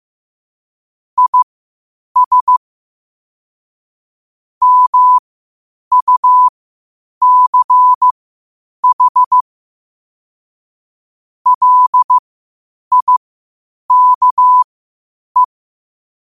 Morse code, heard as a keyed tone, about 15 wpm, spaced as for 8 wpm, 1000 Hz. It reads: IS MUCH LIKE
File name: qotdfiles / 8wpm